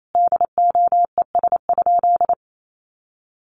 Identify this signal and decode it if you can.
Morse code, heard as a keyed tone, about 28 wpm, 700 Hz. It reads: DOES?
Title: Sentences from Top 1300 Words 28wpm